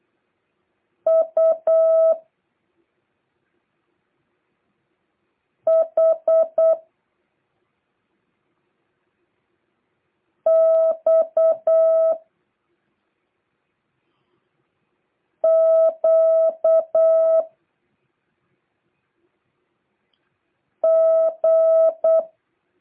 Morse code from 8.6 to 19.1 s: X Q